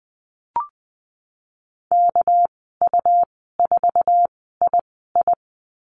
Morse code, heard as a keyed tone, about 20 wpm, spaced as for 15 wpm, 700 Hz. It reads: KU4II
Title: 200 Lesson 32g - Letter k - Call Signs 20wpm